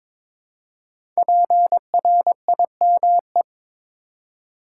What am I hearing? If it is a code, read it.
Morse code, heard as a keyed tone, about 22 wpm, 700 Hz. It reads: PRIME
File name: 5 Letter Words - First 3 Letters Given - 22wpm